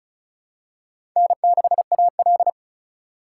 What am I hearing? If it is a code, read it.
Morse code, heard as a keyed tone, about 36 wpm, 700 Hz. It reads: N6AL
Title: USA Call Signs - 4 Characters - 4x Character Spacing - 35wpm